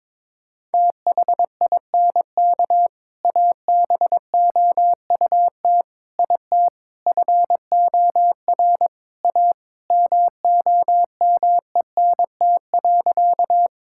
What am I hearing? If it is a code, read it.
Morse code, heard as a keyed tone, about 22 wpm, 700 Hz. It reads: THINK ABOUT IT FOR A MOMENT.
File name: practice / 22wpm / Sentences with 3-Word Phrases - 22wpm